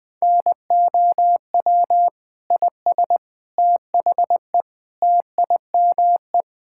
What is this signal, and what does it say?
Morse code, heard as a keyed tone, about 19 wpm, 700 Hz. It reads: NOW IS THE TIME
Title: Pangram Challenge v2 - 20wpm